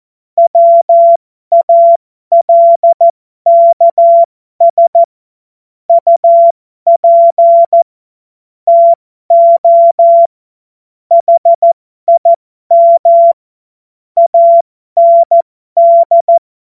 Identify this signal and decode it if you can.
Morse code, heard as a keyed tone, about 14 wpm, spaced as for 12 wpm, 680 Hz. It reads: WALKS UP TO HIM AND
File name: Morsekurs / Witze